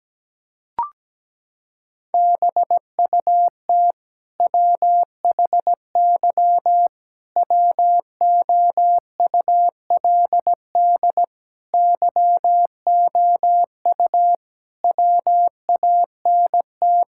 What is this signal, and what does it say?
Morse code, heard as a keyed tone, about 17 wpm, 700 Hz. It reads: BUT WHY WOULD YOU WANT
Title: Sentences from Top 1000 Words - Review Part 6 - 17wpm